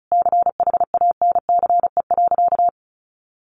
Morse code, heard as a keyed tone, about 34 wpm, 700 Hz. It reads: CHANCE.
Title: Sentences from Top 1100 Words 35wpm